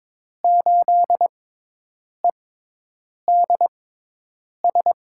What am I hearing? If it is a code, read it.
Morse code, heard as a keyed tone, about 23 wpm, spaced as for 8 wpm, 700 Hz. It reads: 8EDS
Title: USA Call Signs - 5 Characters - 6x Character Spacing - 22wpm